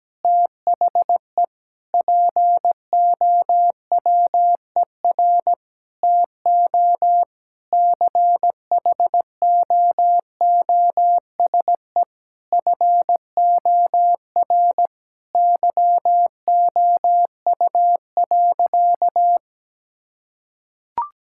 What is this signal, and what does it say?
Morse code, heard as a keyed tone, about 17 wpm, 700 Hz. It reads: THE POWER TO CHOOSE FOR YOU.
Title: Sentences from Top 1000 Words - Review Part 6 - 17wpm